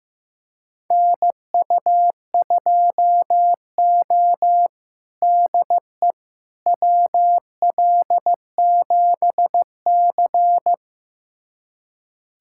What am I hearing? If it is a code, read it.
Morse code, heard as a keyed tone, about 15 wpm, 700 Hz. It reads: NU2O DE WL7C
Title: USA Callsign DE Callsign - 4 Characters - 15wpm